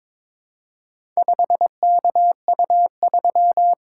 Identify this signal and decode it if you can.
Morse code, heard as a keyed tone, about 22 wpm, 700 Hz. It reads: 5KU3
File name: practice / 22wpm / Four Letter-Number Combinations - 22wpm